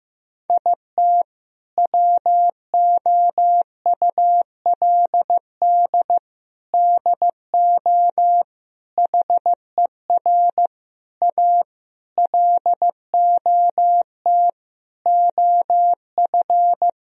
Morse code, heard as a keyed tone, about 16 wpm, 700 Hz. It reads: IT WOULD DO HER A LOT OF
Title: Sentences from Top 1000 Words - Review Part 3 - 15wpm